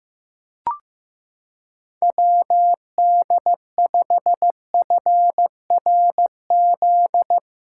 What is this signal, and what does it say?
Morse code, heard as a keyed tone, about 16 wpm, 700 Hz. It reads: WD5FRZ
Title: USA Callsign DE Callsign 15wpm